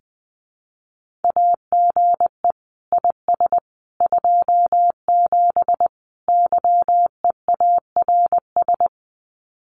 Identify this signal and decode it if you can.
Morse code, heard as a keyed tone, about 20 wpm, 700 Hz. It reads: AGE IS 27 YEARS
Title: Common QSO Phrases Encore 20wpm